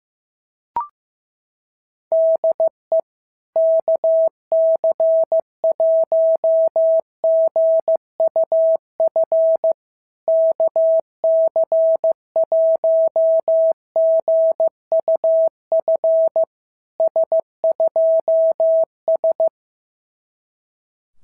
Morse code, heard as a keyed tone, about 15 wpm, 650 Hz. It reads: DE KC1GUF KC1GUF S2S
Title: SOTA QSO Practice 15wpm